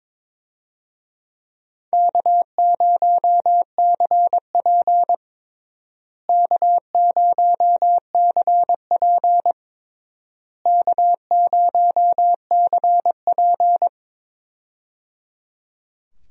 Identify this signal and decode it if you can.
Morse code, heard as a keyed tone, about 22 wpm, 700 Hz. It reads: K0CP K0CP K0CP
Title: USA Call Signs - 4 Characters - Repeated 3x - 22wpm